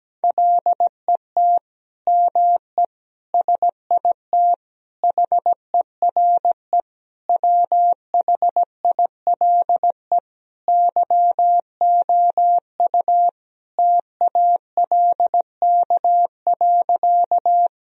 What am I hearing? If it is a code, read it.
Morse code, heard as a keyed tone, about 17 wpm, 700 Hz. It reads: LET ME SIT HERE WHILE YOU TALK.
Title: Sentences from Top 1000 Words - Review Part 1 - 17wpm